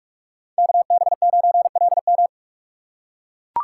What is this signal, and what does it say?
Morse code, heard as a keyed tone, about 44 wpm, 700 Hz. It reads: KB9LM
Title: USA Call Signs - 5 Characters - 6x Character Spacing - 45wpm